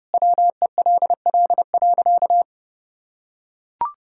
Morse code, heard as a keyed tone, about 30 wpm, 700 Hz. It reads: WELL.
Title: Sentences from Top 1000 Words - Review Part 6 - 30wpm